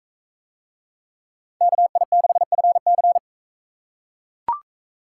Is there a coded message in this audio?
Yes: Morse code from 1.6 to 3.2 s, heard as a keyed tone, about 42 wpm, 700 Hz. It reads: KI6FC